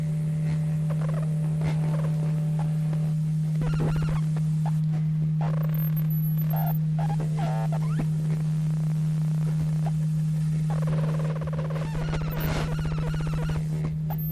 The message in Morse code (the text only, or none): none